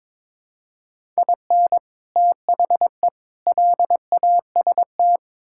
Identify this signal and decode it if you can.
Morse code, heard as a keyed tone, about 21 wpm, 700 Hz. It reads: IN THE LAST
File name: practice / 22wpm / Build A Phrase - 5-Word Phrases - 22wpm